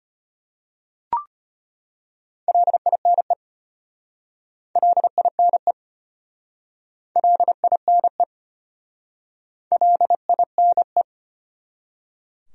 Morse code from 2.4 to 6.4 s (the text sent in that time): LINE LINE